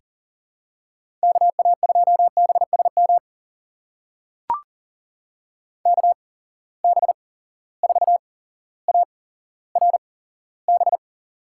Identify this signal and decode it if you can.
Morse code, heard as a keyed tone, about 40 wpm, spaced as for 17 wpm, 700 Hz. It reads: KA2BSM KB4ARB
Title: USA Call Signs - 6 Characters - 8x Character Spacing - 40wpm